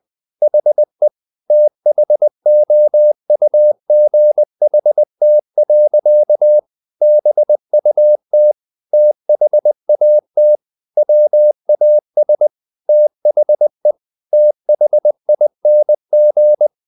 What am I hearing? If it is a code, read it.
Morse code, heard as a keyed tone, about 20 wpm, 600 Hz. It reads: HE THOUGHT. BUT THAT WAS THE THING
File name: the old man and the sea / the old man and the sea part 1 20wpm